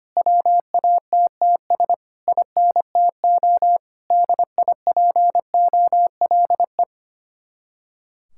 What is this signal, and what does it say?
Morse code, heard as a keyed tone, about 25 wpm, 700 Hz. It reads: WATTS INTO DIPOLE
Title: Speed-Racing Common QSO Phrases - 25wpm